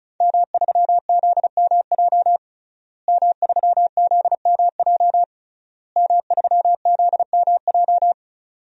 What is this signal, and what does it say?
Morse code, heard as a keyed tone, about 36 wpm, 700 Hz. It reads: M3ZMJ M3ZMJ M3ZMJ
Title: DXCC-100 Call Signs - 5 Characters - Repeated 3x - 35wpm